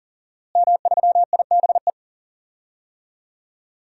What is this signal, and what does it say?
Morse code, heard as a keyed tone, about 39 wpm, 700 Hz. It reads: M3IBE
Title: DXCC-100 Call Signs - 5 Characters - Repeated 3x - 40wpm